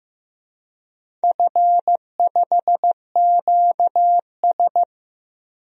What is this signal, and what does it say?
Morse code, heard as a keyed tone, about 15 wpm, 700 Hz. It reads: F5QS
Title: DXCC-100 Call Signs - 4 Characters - 15wpm